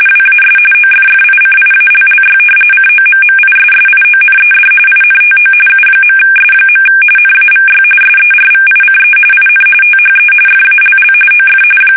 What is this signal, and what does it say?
RTTY, about 45.45 baud. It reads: WAS OFFICIALLY COMMISSIONED INTO U.S. NAVY SERVICE IN THE FALL OF
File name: fsk850 / usnavy3